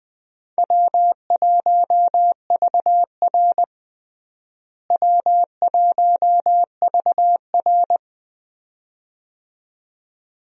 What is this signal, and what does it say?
Morse code, heard as a keyed tone, about 19 wpm, 700 Hz. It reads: W1VR W1VR
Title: USA Call Signs - 4 Characters - Repeated 2x - 20wpm